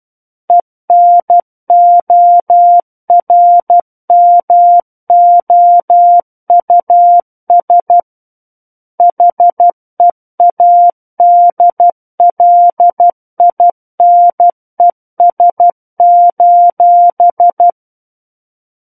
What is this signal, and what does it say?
Morse code, heard as a keyed tone, about 12 wpm, 700 Hz. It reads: ENORMOUS HEADLINES:
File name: Book One / 03-Chapter 3 - 12 WPM